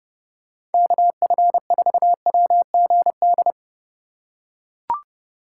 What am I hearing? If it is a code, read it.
Morse code, heard as a keyed tone, about 30 wpm, 700 Hz. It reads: KF4WGD